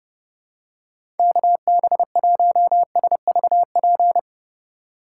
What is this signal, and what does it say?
Morse code, heard as a keyed tone, about 32 wpm, 700 Hz. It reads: KB1SVP
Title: USA Call Signs - 3x Character Spacing - 30wpm